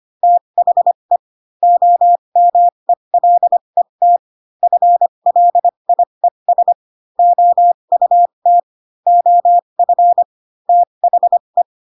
Morse code, heard as a keyed tone, about 25 wpm, spaced as for 22 wpm, 700 Hz. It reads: THE OMELET FLIES OUT OF THE